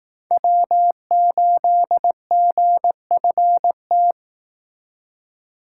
Morse code, heard as a keyed tone, about 18 wpm, 700 Hz. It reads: W8GFT
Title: Speed Racing - USA 1x3 Call Signs 15wpm